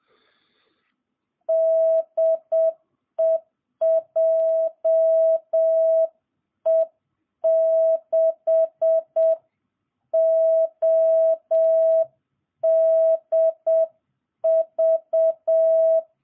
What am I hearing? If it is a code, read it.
Morse code, heard as a keyed tone, about 7 wpm, 660 Hz. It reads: DE JE6ODV